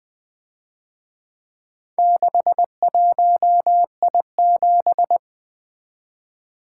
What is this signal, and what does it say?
Morse code, heard as a keyed tone, about 24 wpm, 700 Hz. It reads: 61I7
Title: Speed-Racing - Four Letter-Number Combinations - 20wpm